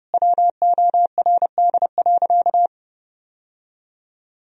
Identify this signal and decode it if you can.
Morse code, heard as a keyed tone, about 30 wpm, 700 Hz. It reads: WORD.
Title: Sentences from Top 1000 Words - Review Part 2 - 30wpm